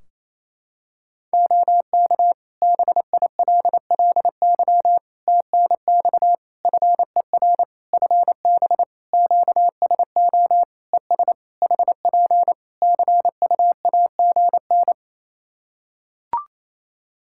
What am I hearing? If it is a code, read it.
Morse code, heard as a keyed tone, about 28 wpm, 700 Hz. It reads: OK BILLY TNX FER FB QSO ES HP CUAGN
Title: Practice QSO - Exchange 4 - Ending - 28wpm